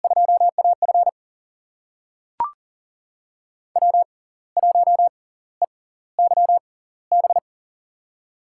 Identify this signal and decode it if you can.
Morse code, heard as a keyed tone, about 40 wpm, spaced as for 18 wpm, 700 Hz. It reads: 2AF W1EYB
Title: USA Call Signs - 5 Characters - 6x Character Spacing - 40wpm